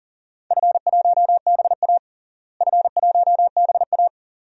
Morse code, heard as a keyed tone, about 40 wpm, 700 Hz. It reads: F1BA F1BA
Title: DXCC-100 Call Signs - 4 Characters - Repeated 2x - 40wpm